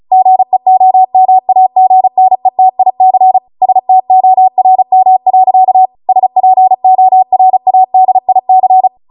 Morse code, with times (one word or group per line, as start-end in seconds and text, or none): 0.1-3.4 GEOMAGNETIC
3.6-5.9 STORM.
6.1-8.9 SPORADIC